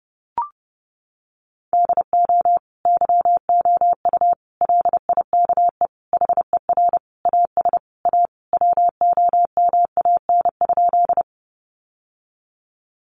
Morse code, heard as a keyed tone, about 30 wpm, 700 Hz. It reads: DO YOU LIKE HER AS A WOMAN?
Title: Sentences from Top 1100 Words 30wpm